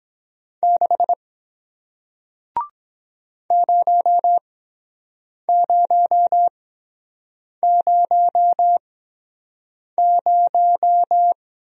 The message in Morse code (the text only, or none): 6 0000